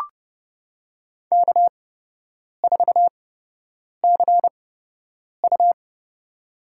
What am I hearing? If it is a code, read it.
Morse code, heard as a keyed tone, about 30 wpm, spaced as for 8 wpm, 700 Hz. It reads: K4CU
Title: USA Call Signs - 4 Characters - 8x Character Spacing - 30wpm